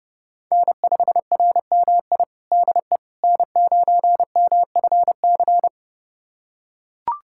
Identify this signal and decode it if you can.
Morse code, heard as a keyed tone, about 30 wpm, 700 Hz. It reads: N5RMI DE N9MFC
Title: USA Callsign DE Callsign - 5 Characters - 30wpm